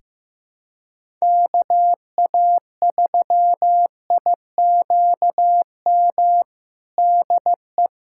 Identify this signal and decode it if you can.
Morse code, heard as a keyed tone, about 15 wpm, 700 Hz. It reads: KA3IQM DE
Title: USA Callsign DE Callsign - 6 Characters - 15wpm